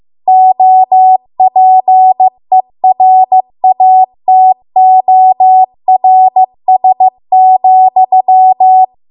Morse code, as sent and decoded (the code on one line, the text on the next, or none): --- .--. .-. .- - --- .-. ... --..--
OPERATORS,